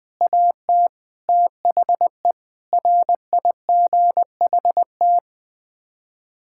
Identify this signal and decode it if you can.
Morse code, heard as a keyed tone, about 20 wpm, 700 Hz. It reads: AT THE RIGHT
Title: Sets of 6 Words - 20wpm